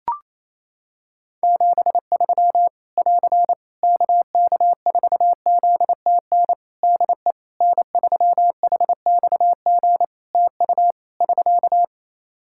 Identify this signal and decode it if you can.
Morse code, heard as a keyed tone, about 28 wpm, 700 Hz. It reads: 73 <AR> KK4ZTN DE N3HXG TU <SK>
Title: Practice QSO - Exchange 4 - Ending - 28wpm